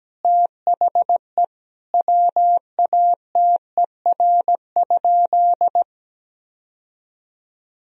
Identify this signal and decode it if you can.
Morse code, heard as a keyed tone, about 17 wpm, 700 Hz. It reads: THE WATER?
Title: Sentences from Top 1000 Words - Review Part 3 - 17wpm